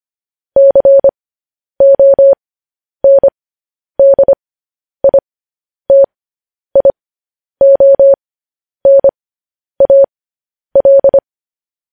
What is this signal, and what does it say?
Morse code, heard as a keyed tone, about 25 wpm, spaced as for 10 wpm, 550 Hz. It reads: CONDITIONAL
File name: podcast / News headlines-1x-25x10